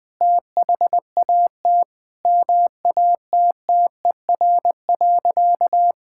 Morse code, heard as a keyed tone, about 20 wpm, 700 Hz. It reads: THAT MATTER.
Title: Sentences with 5-Word Phrases - 20wpm